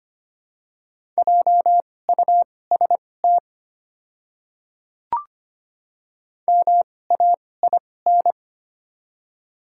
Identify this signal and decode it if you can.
Morse code, heard as a keyed tone, about 25 wpm, spaced as for 18 wpm, 700 Hz. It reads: JUST MAIN
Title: 2x Character Spacing - Top 300 Words - 25wpm